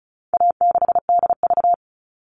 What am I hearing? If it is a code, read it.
Morse code, heard as a keyed tone, about 34 wpm, 700 Hz. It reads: A6DV
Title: All Cut Numbers - 4-Digit 35wpm